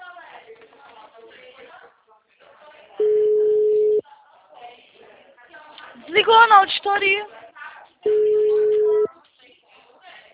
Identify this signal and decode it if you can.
Ringback tone (Europe, 425 Hz), with speech.